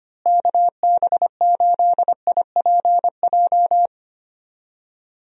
Morse code, heard as a keyed tone, about 25 wpm, 700 Hz. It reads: KB8IPJ
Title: USA Call Signs - 6 Characters - Repeated 3x - 25wpm